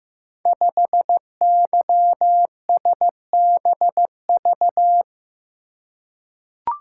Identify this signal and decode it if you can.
Morse code, heard as a keyed tone, about 15 wpm, 700 Hz. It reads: 5YSBV
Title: Code Groups - 5 Letter:Numbers - 15wpm